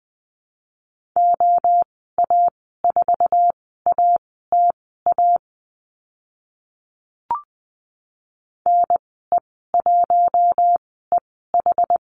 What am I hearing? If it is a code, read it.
Morse code, heard as a keyed tone, about 20 wpm, spaced as for 14 wpm, 700 Hz. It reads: OA4ATA NE1EH
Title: 076 Lesson 13g - Letter L - Call Signs 20wpm